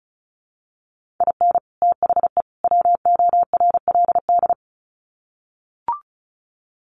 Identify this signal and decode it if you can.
Morse code, heard as a keyed tone, about 34 wpm, 700 Hz. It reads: IN THE WORLD
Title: Sets of 3 Words 35wpm